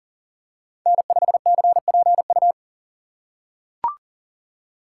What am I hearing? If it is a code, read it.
Morse code, heard as a keyed tone, about 40 wpm, 700 Hz. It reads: N5CPU
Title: USA Call Signs - 2x Character Spacing - 40wpm